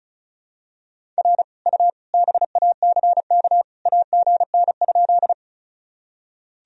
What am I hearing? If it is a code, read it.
Morse code, heard as a keyed tone, about 36 wpm, 700 Hz. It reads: R U BACK AGN?
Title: Sentences from Top 200 Words and Abbreviations 35wpm